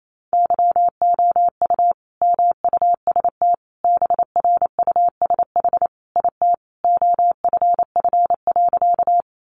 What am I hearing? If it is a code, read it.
Morse code, heard as a keyed tone, about 28 wpm, 700 Hz. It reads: YOU MUST BRUSH IT OFF.
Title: Sentences from Top 1100 Words 28wpm